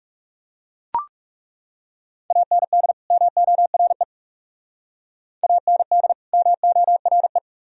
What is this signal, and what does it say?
Morse code, heard as a keyed tone, about 38 wpm, 700 Hz. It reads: AND MORE AND MORE